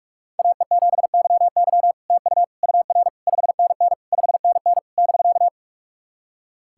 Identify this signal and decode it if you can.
Morse code, heard as a keyed tone, about 43 wpm, 700 Hz. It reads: AE7YY TU UR 5NN 5NN <BK>
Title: SOTA QSO Practice 45wpm